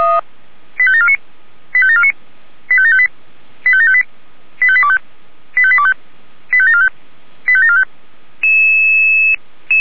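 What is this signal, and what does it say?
Five-tone selective call (ZVEI), heard as quick runs of five tones, tones of about 70 ms. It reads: 85749 85749 85758 85758 85825 85825 85745 85745
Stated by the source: Five-tone ZVEI selective call 85730, received on an SDR.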